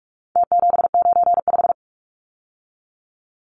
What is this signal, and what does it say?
Morse code, heard as a keyed tone, about 44 wpm, 700 Hz. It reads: T795
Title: Numbers - Contest Style - 4-Digit 45wpm